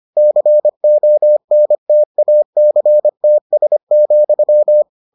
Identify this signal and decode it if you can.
Morse code, heard as a keyed tone, about 25 wpm, 600 Hz. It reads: CONTACTS,